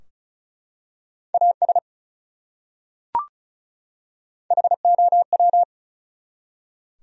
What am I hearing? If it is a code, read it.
Morse code, heard as a keyed tone, about 36 wpm, 700 Hz. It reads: AS HOW
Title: Top 100 Words 35wpm